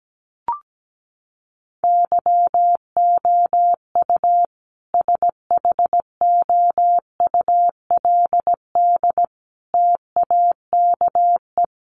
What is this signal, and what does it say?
Morse code, heard as a keyed tone, about 17 wpm, 700 Hz. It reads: YOU SHOULD TAKE
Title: Sentences from Top 300 Words 17wpm